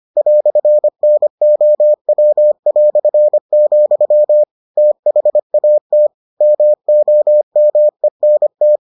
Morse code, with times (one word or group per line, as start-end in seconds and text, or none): 0.2-4.4 "NOW",
4.8-6.1 THAT
6.4-8.8 MOMENT